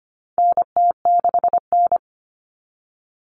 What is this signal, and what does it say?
Morse code, heard as a keyed tone, about 25 wpm, 700 Hz. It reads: NT6N
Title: All Cut Numbers - 4-Digit 25wpm